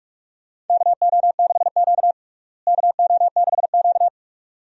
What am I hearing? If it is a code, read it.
Morse code, heard as a keyed tone, about 44 wpm, 700 Hz. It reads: KO6Q KO6Q
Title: USA Call Signs - 4 Characters - Repeated 2x - 45wpm